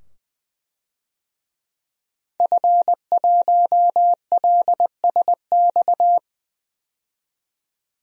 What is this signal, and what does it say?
Morse code, heard as a keyed tone, about 20 wpm, 700 Hz. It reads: F1LSX
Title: DXCC-100 Call Signs - 5 Characters - 20wpm